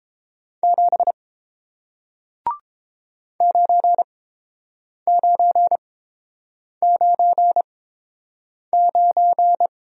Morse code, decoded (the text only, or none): 7 9999